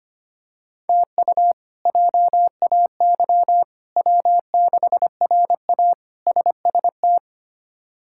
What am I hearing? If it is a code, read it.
Morse code, heard as a keyed tone, about 25 wpm, 700 Hz. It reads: TU JAY W6RA SST
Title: K1USN SST QSO Practice - 25wpm